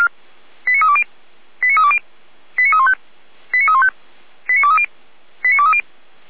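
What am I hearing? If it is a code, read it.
Five-tone selective call (ZVEI), heard as quick runs of five tones, tones of about 70 ms. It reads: 89320 89320 89315 89315 89230 89230
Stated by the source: Five-tone ZVEI selective call received on an SDR, recorded by feuerwehr-orlamuende.org (fire brigade Orlamünde).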